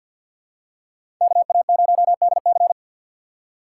Morse code, heard as a keyed tone, about 51 wpm, 700 Hz. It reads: KA0DC
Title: USA Call Signs - 5 Characters - 4x Character Spacing - 50wpm